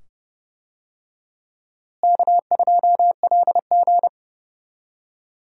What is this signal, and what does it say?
Morse code, heard as a keyed tone, about 30 wpm, 700 Hz. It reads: K2LG
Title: USA Call Signs - 4 Characters - Repeated 2x - 30wpm